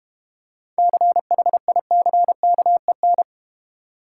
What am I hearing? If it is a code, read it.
Morse code, heard as a keyed tone, about 31 wpm, 700 Hz. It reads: CHICKEN
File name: Easy Words - Speed-Racing with Hints - 25wpm